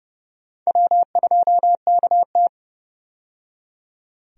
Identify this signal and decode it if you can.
Morse code, heard as a keyed tone, about 32 wpm, 700 Hz. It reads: W2KT
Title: USA Call Signs - 4 Characters - 30wpm